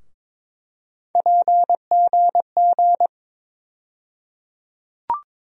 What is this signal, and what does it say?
Morse code, heard as a keyed tone, about 22 wpm, 700 Hz. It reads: PGG
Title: USA Call Sign Postfixes - 3 Letters 22wpm